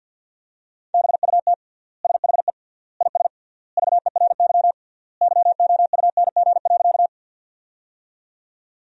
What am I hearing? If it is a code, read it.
Morse code, heard as a keyed tone, about 50 wpm, 700 Hz. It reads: BUT SHE IS VERY YOUNG.
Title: 3x Word-Spacing - Sentences from Top 400 Words - 50wpm